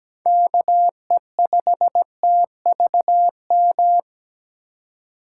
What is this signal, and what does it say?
Morse code, heard as a keyed tone, about 17 wpm, 700 Hz. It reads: KE5TVM
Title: USA Call Signs - 4x Character Spacing - 17wpm